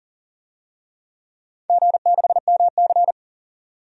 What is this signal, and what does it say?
Morse code, heard as a keyed tone, about 41 wpm, 700 Hz. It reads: G6MC
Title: DXCC-100 Call Signs - 4 Characters - 40wpm